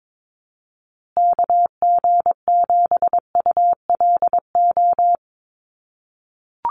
Morse code, heard as a keyed tone, about 22 wpm, 700 Hz. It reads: KG7ULO
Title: USA Call Signs Repeated 3x 22wpm